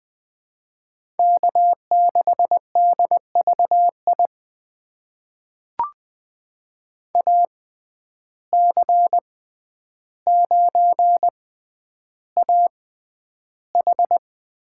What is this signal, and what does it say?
Morse code, heard as a keyed tone, about 20 wpm, spaced as for 10 wpm, 700 Hz. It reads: K6DVI AC9AH
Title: USA Call Signs - 5 Characters - 6x Character Spacing - 20wpm